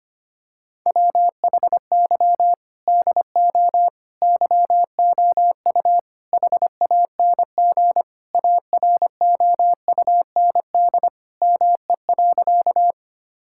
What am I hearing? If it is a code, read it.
Morse code, heard as a keyed tone, about 25 wpm, 700 Hz. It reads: WHY DO YOU HANG AROUND ME.
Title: Sentences from Top 1600 Words 25wpm